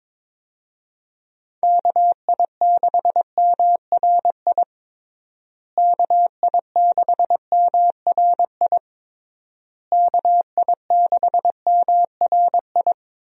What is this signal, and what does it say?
Morse code, heard as a keyed tone, about 22 wpm, 700 Hz. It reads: KI6MRI KI6MRI KI6MRI
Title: USA Call Signs - 6 Characters - Repeated 3x - 22wpm